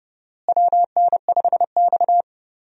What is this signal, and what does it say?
Morse code, heard as a keyed tone, about 29 wpm, 700 Hz. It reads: WN5X
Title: USA Call Signs - 4 Characters - Repeated 3x - 30wpm